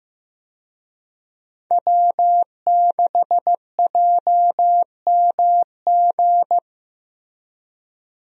Morse code, heard as a keyed tone, about 15 wpm, 700 Hz. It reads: W6JMG
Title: USA Call Signs - 5 Characters - Repeated 2x - 15wpm